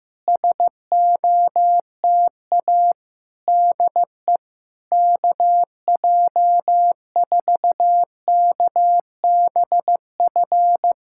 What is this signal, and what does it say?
Morse code, heard as a keyed tone, about 15 wpm, 700 Hz. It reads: SOTA DE KJ4KBF
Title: SOTA QSO Practice 15wpm